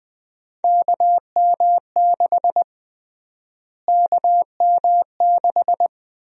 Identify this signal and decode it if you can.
Morse code, heard as a keyed tone, about 20 wpm, 700 Hz. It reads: KM6 KM6
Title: USA Call Sign Prefixes - Repeated 2x - 20wpm